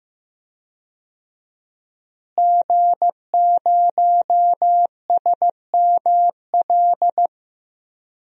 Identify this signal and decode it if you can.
Morse code, heard as a keyed tone, about 15 wpm, 700 Hz. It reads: G0SML
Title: DXCC-100 Call Signs - 5 Characters - Repeated 2x - 15wpm